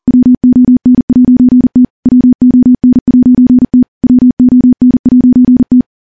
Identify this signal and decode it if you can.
Morse code, heard as a keyed tone, about 43 wpm, 250 Hz. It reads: WON'T WON'T WON'T
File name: morse / morse-training-40WPM